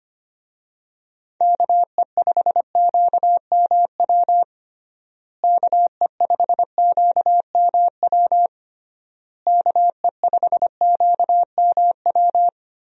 Morse code, heard as a keyed tone, about 25 wpm, 700 Hz. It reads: KE5QMW KE5QMW KE5QMW
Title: USA Call Signs - 6 Characters - Repeated 3x - 25wpm